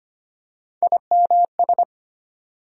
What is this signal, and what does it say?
Morse code, heard as a keyed tone, about 25 wpm, 700 Hz. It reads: IMS
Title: USA Call Sign Postfixes - 3 Letters 25wpm